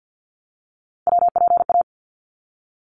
Morse code, heard as a keyed tone, about 50 wpm, 700 Hz. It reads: WPA
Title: ARRL Sections 50wpm